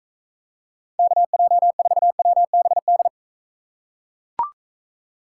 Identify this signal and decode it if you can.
Morse code, heard as a keyed tone, about 42 wpm, 700 Hz. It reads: KJ4WBD